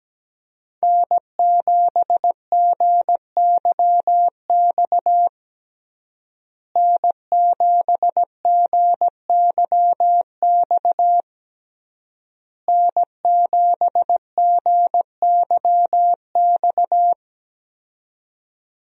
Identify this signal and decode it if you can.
Morse code, heard as a keyed tone, about 17 wpm, 700 Hz. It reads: N7GYX N7GYX N7GYX
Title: USA Call Signs - 5 Characters - Repeated 3x - 17wpm